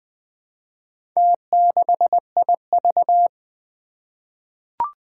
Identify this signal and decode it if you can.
Morse code, heard as a keyed tone, about 20 wpm, 700 Hz. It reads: T6IV
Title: Four Challenging Letter-Number Combinations 20wpm